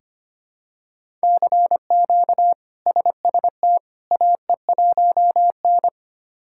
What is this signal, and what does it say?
Morse code, heard as a keyed tone, about 25 wpm, 700 Hz. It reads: CQ SST AE1N
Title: K1USN SST QSO Practice - 25wpm